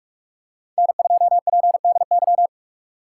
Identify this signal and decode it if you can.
Morse code, heard as a keyed tone, about 42 wpm, 700 Hz. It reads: N2PDY
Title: USA Call Signs - 5 Characters - 3x Character Spacing - 45wpm